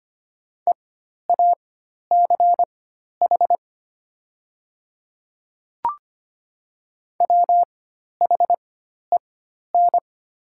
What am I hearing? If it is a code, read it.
Morse code, heard as a keyed tone, about 25 wpm, spaced as for 12 wpm, 700 Hz. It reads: EACH WHEN
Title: 4x Character Spacing - Top 100 Words - 25wpm